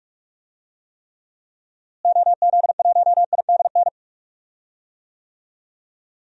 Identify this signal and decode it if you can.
Morse code, heard as a keyed tone, about 46 wpm, 700 Hz. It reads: OZ1IDN
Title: DXCC-100 Call Signs - 6 Characters - Repeated 3x - 45wpm